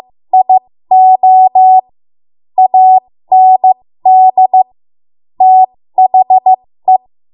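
Morse code, heard as a keyed tone, about 15 wpm, spaced as for 13 wpm, 750 Hz. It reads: IO AND THE
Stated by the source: 13 wpm Morse code practice archive transmission.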